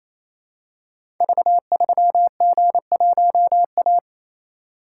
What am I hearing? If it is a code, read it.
Morse code, heard as a keyed tone, about 28 wpm, 700 Hz. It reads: V3G1A
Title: Code Groups - 5 Letter:Numbers - 28wpm